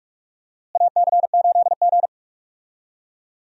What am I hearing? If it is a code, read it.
Morse code, heard as a keyed tone, about 41 wpm, 700 Hz. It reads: AC8G
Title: USA Call Signs - 4 Characters - Repeated 2x - 45wpm